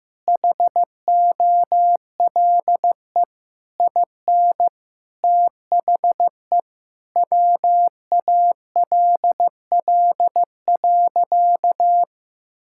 Morse code, heard as a keyed tone, about 16 wpm, 700 Hz. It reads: HOLE IN THE WALL.
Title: Sentences from Top 1600 Words 15wpm